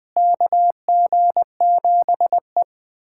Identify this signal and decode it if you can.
Morse code, heard as a keyed tone, about 20 wpm, 700 Hz. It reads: KG7E
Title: USA Callsign DE Callsign - 4 Characters - 20wpm